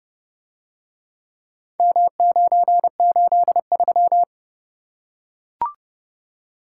Morse code, heard as a keyed tone, about 29 wpm, 700 Hz. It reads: M983